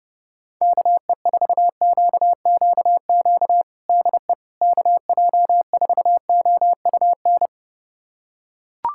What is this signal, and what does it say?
Morse code, heard as a keyed tone, about 30 wpm, 700 Hz. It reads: KE4QQQ DE KJ4OUN
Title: USA Callsign DE Callsign - 6 Characters - 30wpm